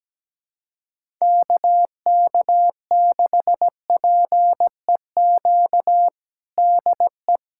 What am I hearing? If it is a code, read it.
Morse code, heard as a keyed tone, about 17 wpm, 700 Hz. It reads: KK6PEQ DE
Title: USA Callsign DE Callsign - 6 Characters - 17wpm